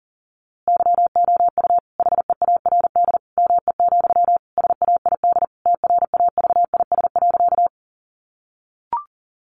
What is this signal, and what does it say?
Morse code, heard as a keyed tone, about 37 wpm, 700 Hz. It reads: YOU HEARD ME, SAID TRAVIS.
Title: A Sound Of Thunder 40wpm